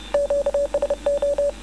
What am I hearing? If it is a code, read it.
Morse code, heard as a keyed tone, about 30 wpm, 570 Hz. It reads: QSO